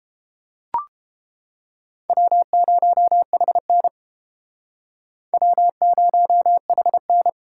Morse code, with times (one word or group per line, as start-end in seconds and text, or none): 2.1-3.9 W0HN
5.3-7.3 W0HN